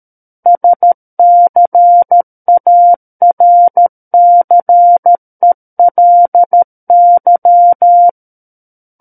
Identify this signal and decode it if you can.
Morse code, heard as a keyed tone, about 13 wpm, 700 Hz. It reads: SCARCELY
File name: Book One / 04-Chapter 4 - 13 WPM